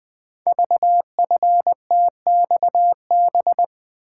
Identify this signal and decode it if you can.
Morse code, heard as a keyed tone, about 20 wpm, 700 Hz. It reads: VFTXB
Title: Code Groups - 5 Letters - 20wpm